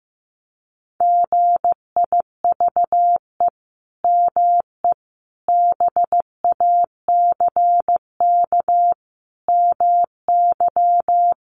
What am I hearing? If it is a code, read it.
Morse code, heard as a keyed tone, about 15 wpm, 700 Hz. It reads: GIVE ME BACK MY
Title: Sentences from Top 800 Words 15wpm